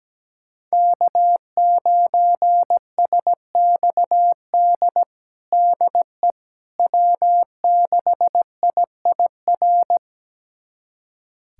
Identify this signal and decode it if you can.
Morse code, heard as a keyed tone, about 17 wpm, 700 Hz. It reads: K9SXD DE W6IIR
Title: USA Callsign DE Callsign - 5 Characters - 17wpm